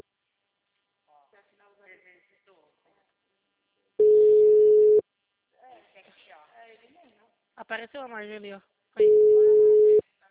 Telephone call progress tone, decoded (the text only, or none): ringback tone